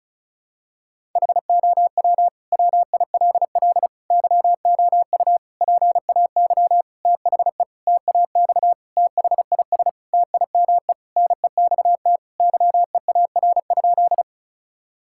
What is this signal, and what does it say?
Morse code, heard as a keyed tone, about 36 wpm, 700 Hz. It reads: HOW WILL YOU PAY THE TAX THIS TIME NEXT YEAR?
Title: Sentences from Top 1000 Words - Review Part 5 - 35wpm